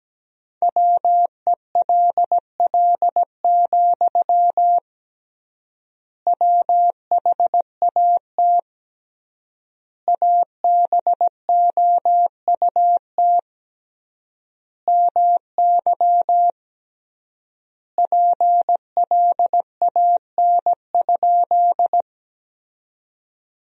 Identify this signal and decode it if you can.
Morse code, heard as a keyed tone, about 17 wpm, 700 Hz. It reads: WELL, WHAT ABOUT MY PLAN?
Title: 3x Word-Spacing - Sentences from Top 500 Words - 17wpm